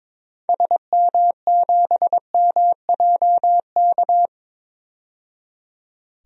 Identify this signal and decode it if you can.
Morse code, heard as a keyed tone, about 23 wpm, 700 Hz. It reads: SM7MJK
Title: DXCC-100 Call Signs - 6 Characters - 22wpm